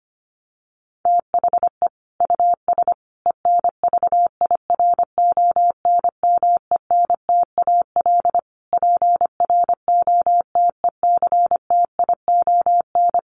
Morse code, heard as a keyed tone, about 25 wpm, 700 Hz. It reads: THE US ENVIRONMENTAL PROTECTION
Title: Sets of 5 Words Encore2 25wpm